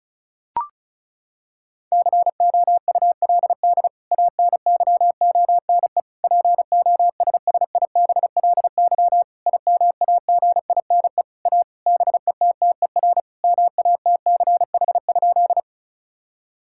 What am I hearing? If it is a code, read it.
Morse code, heard as a keyed tone, about 36 wpm, 700 Hz. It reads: COULD ANYONE POSSIBLY IMAGINE A BETTER MATCH?
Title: Sentences from Top 1600 Words 35wpm